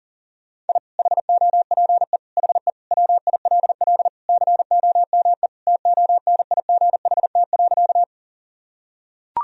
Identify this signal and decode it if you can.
Morse code, heard as a keyed tone, about 37 wpm, 700 Hz. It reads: I HOPE HE WILL COME TONIGHT.
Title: Sentences from Top 1000 Words - Review Part 3 - 40wpm